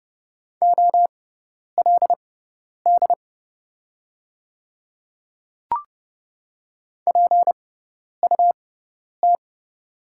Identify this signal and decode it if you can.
Morse code, heard as a keyed tone, about 30 wpm, spaced as for 10 wpm, 690 Hz. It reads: OLD PUT
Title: 6x Character Spacing - Top 500 Words - 30wpm